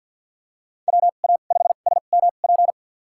Morse code, heard as a keyed tone, about 52 wpm, spaced as for 36 wpm, 700 Hz. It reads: WA5SMP
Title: USA Call Signs - 2x Character Spacing - 50wpm